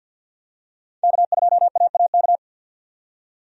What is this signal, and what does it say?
Morse code, heard as a keyed tone, about 47 wpm, 700 Hz. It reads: K2AAK
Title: USA Call Signs - 5 Characters - 50wpm